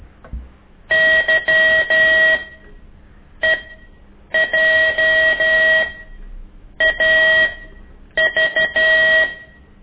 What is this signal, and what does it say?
Morse code, heard as a keyed tone, about 12 wpm, spaced as for 7 wpm, 1900 Hz. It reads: YEJAV
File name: ik1weg2 / CORSI CW